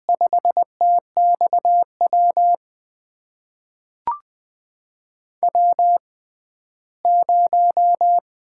Morse code, heard as a keyed tone, about 20 wpm, spaced as for 14 wpm, 700 Hz. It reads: 5TXW W0
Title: USA Call Signs - 5 Characters - 6x Character Spacing - 20wpm